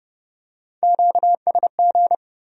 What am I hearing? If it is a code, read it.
Morse code, heard as a keyed tone, about 25 wpm, 700 Hz. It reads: QSG
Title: Speed-Racing Q-Codes - 20wpm